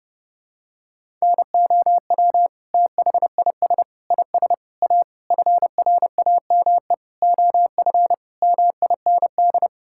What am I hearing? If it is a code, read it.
Morse code, heard as a keyed tone, about 29 wpm, 700 Hz. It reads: NOW THIS IS A FRAME OF MIND
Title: Sentences from Top 1000 Words - Review Part 6 - 30wpm